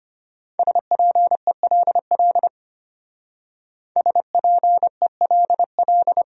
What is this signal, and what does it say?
Morse code, heard as a keyed tone, about 32 wpm, 700 Hz. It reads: SPELL SPELL